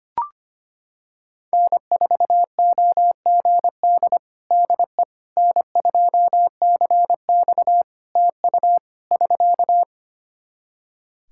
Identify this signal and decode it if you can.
Morse code, heard as a keyed tone, about 25 wpm, 700 Hz. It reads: N4OGD DE N2CX TU <SK>
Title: Practice QSO - Exchange 4 - Ending - 25wpm